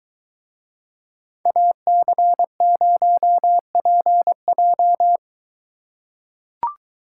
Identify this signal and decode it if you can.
Morse code, heard as a keyed tone, about 22 wpm, 700 Hz. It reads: AC0PJ